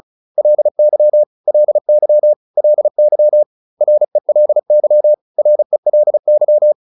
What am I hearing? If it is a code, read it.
Morse code, heard as a keyed tone, about 36 wpm, 600 Hz. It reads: LY LY LY RELY RELY